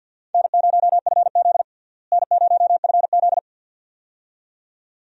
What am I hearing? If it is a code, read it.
Morse code, heard as a keyed tone, about 50 wpm, 700 Hz. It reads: N0FZ N0FZ
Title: USA Call Signs - 4 Characters - Repeated 2x - 50wpm